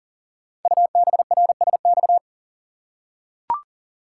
Morse code, heard as a keyed tone, about 39 wpm, 690 Hz. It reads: UBRSX
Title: Code Groups - 5 Letters - 40wpm